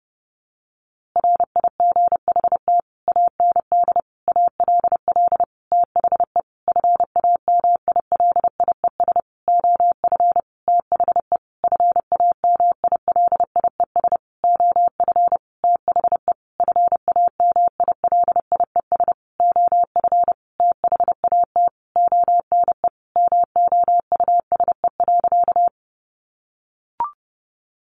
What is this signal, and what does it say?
Morse code, heard as a keyed tone, about 30 wpm, 700 Hz. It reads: RIGHT AND ALL THE FAMILIES OF THE FAMILIES OF THE FAMILIES OF THAT ONE MOUSE.
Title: A Sound Of Thunder 30wpm